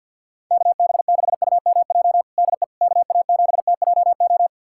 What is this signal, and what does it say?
Morse code, heard as a keyed tone, about 49 wpm, 700 Hz. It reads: KB6UMJ DE KA7TJO
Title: USA Callsign DE Callsign - 6 Characters - 50wpm